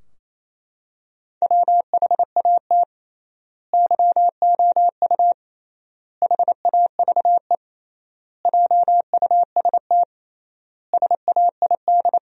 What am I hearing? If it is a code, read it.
Morse code, heard as a keyed tone, about 28 wpm, 700 Hz. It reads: WHAT YOU HAVE JUST SAID
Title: 3x Word-Spacing - Sentences from Top 200 Words - 28wpm